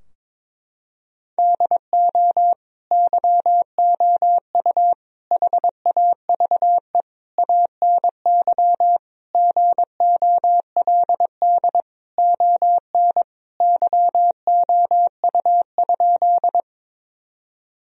Morse code, heard as a keyed tone, about 22 wpm, 700 Hz. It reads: DO YOU HAVE ANY GOLD ON YOU?
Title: Sentences from Top 1000 Words - Review Part 3 - 22wpm